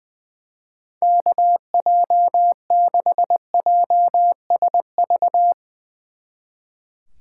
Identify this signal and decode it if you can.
Morse code, heard as a keyed tone, about 20 wpm, 700 Hz. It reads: KJ6JSV
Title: Speed Racing - USA 2x3 Call Signs 20wpm